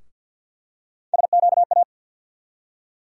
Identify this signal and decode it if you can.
Morse code, heard as a keyed tone, about 52 wpm, 700 Hz. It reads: SQA
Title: USA Call Sign Postfixes - 3 Letters 50wpm